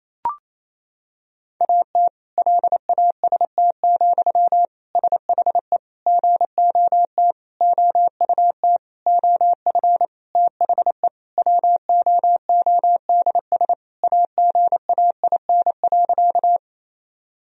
Morse code, heard as a keyed tone, about 28 wpm, 700 Hz. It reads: AT LAST, SHE GOT OUT OF THE WOODS AGAIN.
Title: Sentences from Top 1000 Words - Review Part 2 - 28wpm